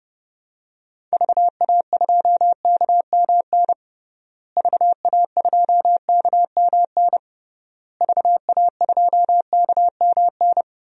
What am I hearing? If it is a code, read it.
Morse code, heard as a keyed tone, about 29 wpm, 700 Hz. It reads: VA2KMN VA2KMN VA2KMN
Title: DXCC-100 Call Signs - 6 Characters - Repeated 3x - 30wpm